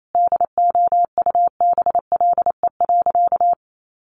Morse code, heard as a keyed tone, about 28 wpm, 700 Hz. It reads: DOUBLE.
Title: Sentences from Top 1400 Words 28wpm